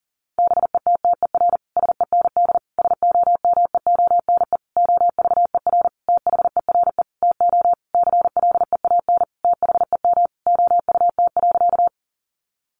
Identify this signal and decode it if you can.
Morse code, heard as a keyed tone, about 40 wpm, 700 Hz. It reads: BETTER SEND SOMEONE OVER THERE TO CLEAN THEM OUT.
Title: Sentences from Top 1500 Words 40wpm